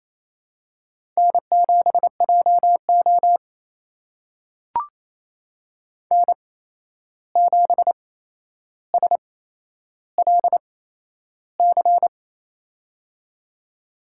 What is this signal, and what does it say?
Morse code, heard as a keyed tone, about 28 wpm, spaced as for 11 wpm, 700 Hz. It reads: N7JO N7SLC
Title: USA Call Signs - 8x Character Spacing - 28wpm